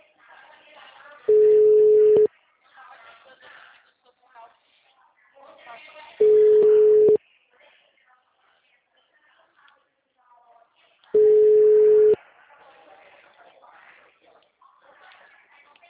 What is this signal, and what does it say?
Ringback tone (Europe, 425 Hz), with speech.